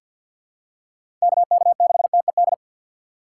Morse code, heard as a keyed tone, about 47 wpm, 700 Hz. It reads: KK6TED
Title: USA Call Signs - 6 Characters - 6x Character Spacing - 50wpm